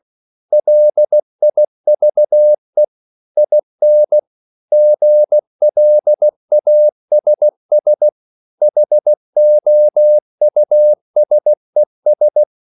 Morse code, heard as a keyed tone, about 16 wpm, 600 Hz. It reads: LIVE IN GLASS HOUSES
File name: CW/Practice